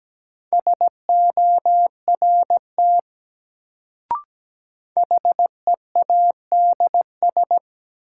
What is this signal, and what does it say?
Morse code, heard as a keyed tone, about 17 wpm, 700 Hz. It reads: SORT HEADS
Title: Sentences from Top 1000 Words - Review Part 3 - New Word Drill - 17wpm